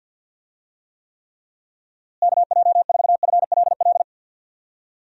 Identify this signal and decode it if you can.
Morse code, heard as a keyed tone, about 51 wpm, 700 Hz. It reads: KJ4FLL
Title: USA Call Signs - 6 Characters - Repeated 3x - 50wpm